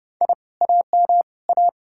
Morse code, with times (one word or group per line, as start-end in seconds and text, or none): 0.2-0.3 I
0.6-1.2 AM
1.5-1.7 A